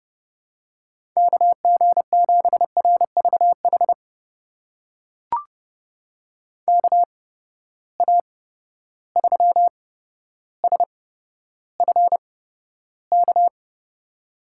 Morse code, heard as a keyed tone, about 30 wpm, spaced as for 13 wpm, 700 Hz. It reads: KG7RVH KA3SFK
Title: USA Call Signs - 6 Characters - 8x Character Spacing - 30wpm